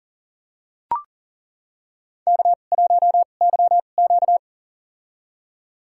Morse code, heard as a keyed tone, about 40 wpm, spaced as for 29 wpm, 700 Hz. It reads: K1YQ
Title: USA Call Signs - 4 Characters - 2x Character Spacing - 40wpm